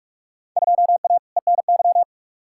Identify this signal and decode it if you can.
Morse code, heard as a keyed tone, about 44 wpm, 700 Hz. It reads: 2A ENY
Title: Field Day Exchanges - 45wpm